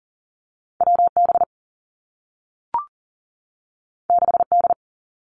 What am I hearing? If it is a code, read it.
Morse code, heard as a keyed tone, about 42 wpm, 700 Hz. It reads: W B 6 D
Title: Two Challenging Letter-Number Combinations 40wpm